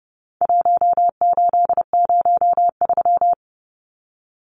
Morse code, heard as a keyed tone, about 30 wpm, 700 Hz. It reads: 1803
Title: Numbers - 4-Digit 30wpm